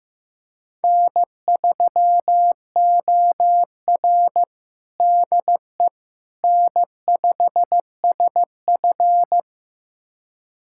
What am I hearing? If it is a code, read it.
Morse code, heard as a keyed tone, about 15 wpm, 700 Hz. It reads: N3OR DE N5SF
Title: USA Callsign DE Callsign - 4 Characters - 15wpm